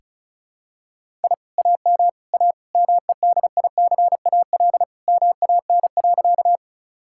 Morse code, heard as a keyed tone, about 36 wpm, 700 Hz. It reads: I AM A MEDICAL MAN.
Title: Sentences from Top 1000 Words - Review Part 5 - 35wpm